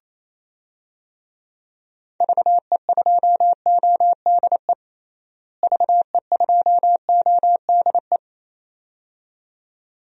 Morse code, heard as a keyed tone, about 28 wpm, 700 Hz. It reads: VE2ODE VE2ODE
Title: DXCC-100 Call Signs - 6 Characters - Repeated 2x - 28wpm